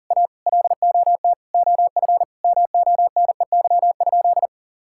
Morse code, heard as a keyed tone, about 37 wpm, 700 Hz. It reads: A LOT OF MONEY?
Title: Sentences from Top 1000 Words - Review Part 5 - 40wpm